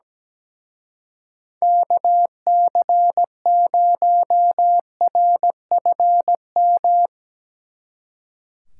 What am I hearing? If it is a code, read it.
Morse code, heard as a keyed tone, about 19 wpm, 700 Hz. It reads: KC0RFM
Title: Speed Racing - USA 2x3 Call Signs 17wpm